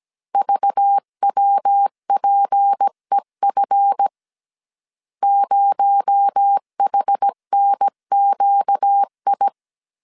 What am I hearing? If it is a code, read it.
Morse code, heard as a keyed tone, about 17 wpm, 800 Hz. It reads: VWPEF 0HNQI
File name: CW / DET / 1 9 1500C S B 16wpm 2856